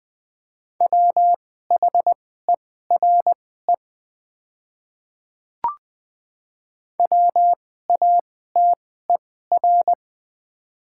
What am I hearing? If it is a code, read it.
Morse code, heard as a keyed tone, about 20 wpm, spaced as for 15 wpm, 690 Hz. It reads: WHERE WATER
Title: 2x Character Spacing - Top 200 Words - 20wpm